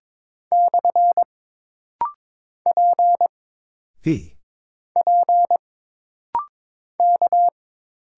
Morse code, heard as a keyed tone, about 22 wpm, 700 Hz. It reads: / P P K
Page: This rapid-fire timing is geared to push your ICR (Instant Character Recognition) skills.